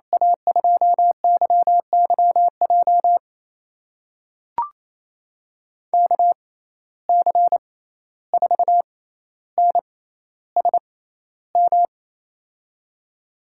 Morse code, heard as a keyed tone, about 28 wpm, spaced as for 14 wpm, 700 Hz. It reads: A2YYJ KC4NSM
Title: USA Call Signs - 6 Characters - 6x Character Spacing - 28wpm